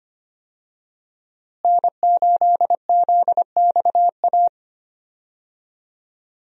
Morse code, heard as a keyed tone, about 25 wpm, 700 Hz. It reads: N8ZXA
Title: USA Call Signs - 5 Characters - Repeated 3x - 25wpm